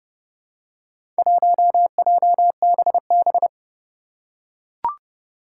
Morse code, heard as a keyed tone, about 30 wpm, 700 Hz. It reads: 1JBB
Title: Four Challenging Letter-Number Combinations 30wpm